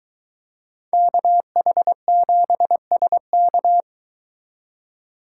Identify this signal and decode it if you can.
Morse code, heard as a keyed tone, about 23 wpm, 700 Hz. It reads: KH7SK